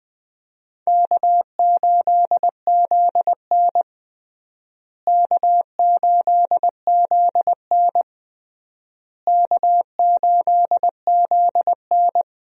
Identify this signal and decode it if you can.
Morse code, heard as a keyed tone, about 20 wpm, 700 Hz. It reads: K8ZN K8ZN K8ZN
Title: USA Call Signs - 4 Characters - Repeated 3x - 20wpm